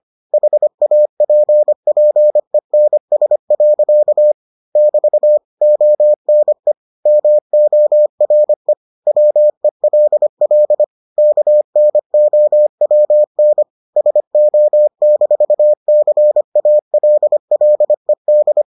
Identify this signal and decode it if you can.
Morse code, heard as a keyed tone, about 25 wpm, 600 Hz. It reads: HAPPENS. <BT> ONE MORE WELL KNOWN SO-CALLED